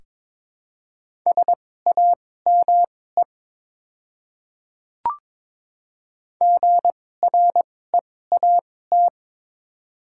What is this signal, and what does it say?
Morse code, heard as a keyed tone, about 22 wpm, spaced as for 16 wpm, 700 Hz. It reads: SAME GREAT
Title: 2x Character Spacing - Top 300 Words - 22wpm